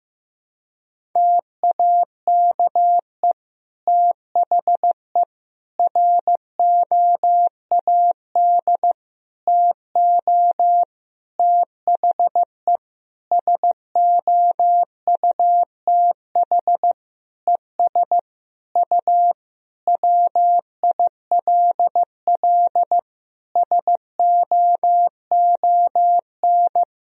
Morse code, heard as a keyed tone, about 15 wpm, 700 Hz. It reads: TAKE THE ROAD TO THE SOUTH ES U WILL SOON
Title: Sentences from Top 400 Words and Abbreviations 15wpm